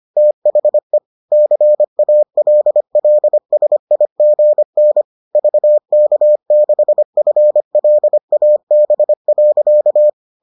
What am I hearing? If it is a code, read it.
Morse code, heard as a keyed tone, about 25 wpm, 600 Hz. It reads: THE CALLSIGN VK6FLAB.